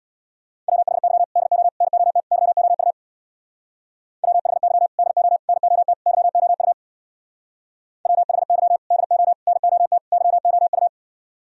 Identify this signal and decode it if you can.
Morse code, heard as a keyed tone, about 65 wpm, 690 Hz. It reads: WHY DO NOT YOU WHY DO NOT YOU WHY DO NOT YOU